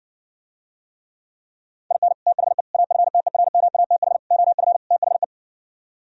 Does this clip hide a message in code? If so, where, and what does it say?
Yes: Morse code from 1.9 to 5.2 s, heard as a keyed tone, about 68 wpm, 700 Hz. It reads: IN THE AFTERMATH OF THE